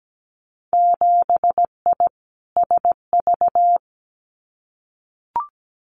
Morse code, heard as a keyed tone, about 17 wpm, 700 Hz. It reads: 7I SV
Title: Winter Field Day Exchanges Encore 17wpm